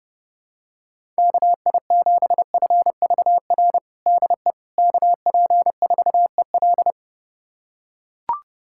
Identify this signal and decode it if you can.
Morse code, heard as a keyed tone, about 30 wpm, 700 Hz. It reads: KI7FVR DE KP4EL
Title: USA Callsign DE Callsign 30wpm